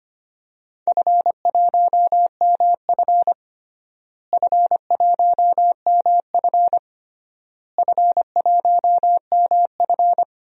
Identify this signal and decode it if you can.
Morse code, heard as a keyed tone, about 25 wpm, 700 Hz. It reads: F1MF F1MF F1MF
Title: DXCC-100 Call Signs - 4 Characters - Repeated 3x - 25wpm